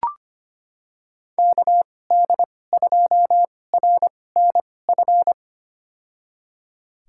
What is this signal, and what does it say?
Morse code, heard as a keyed tone, about 25 wpm, spaced as for 18 wpm, 700 Hz. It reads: KD2RNF
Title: USA Call Signs - 6 Characters - 2x Character Spacing - 25wpm